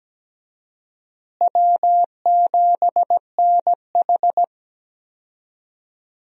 Morse code, heard as a keyed tone, about 17 wpm, 700 Hz. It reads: W7NH
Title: USA Call Signs - 4 Characters - Repeated 3x - 17wpm